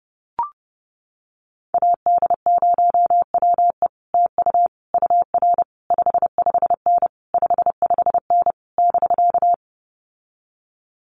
Morse code, heard as a keyed tone, about 30 wpm, 700 Hz. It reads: AD0WE TU UR 55N 55N <BK>
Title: Common QSO Phrases Encore 30wpm